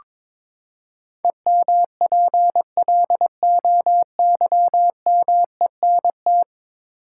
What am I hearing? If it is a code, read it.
Morse code, heard as a keyed tone, about 22 wpm, 700 Hz. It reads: EMPLOYMENT
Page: morse-code-ninja-practice-22wpm